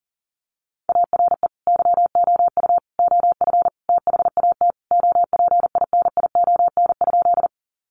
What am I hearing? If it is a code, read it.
Morse code, heard as a keyed tone, about 40 wpm, 700 Hz. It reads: ARE YOU OF THAT OPINION?
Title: Sentences from Top 1200 Words 40wpm